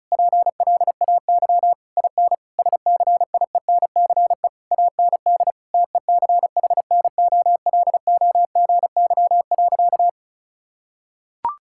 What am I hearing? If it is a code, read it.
Morse code, heard as a keyed tone, about 36 wpm, 700 Hz. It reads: PLAY IN SCIENCE AND TECHNOLOGY.
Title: Sentences from Top 1000 Words - Review Part 6 - 35wpm